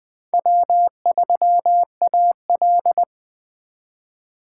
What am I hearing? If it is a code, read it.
Morse code, heard as a keyed tone, about 20 wpm, 700 Hz. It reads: W3AL
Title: USA Call Signs - 4 Characters - 3x Character Spacing - 20wpm